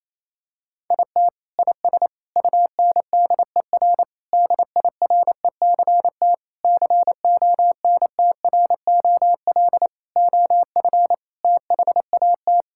Morse code, heard as a keyed tone, about 28 wpm, 700 Hz. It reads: IT IS UNDER DIRECT CONTROL OF THAT
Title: Sentences from Top 1000 Words - Review Part 5 - 28wpm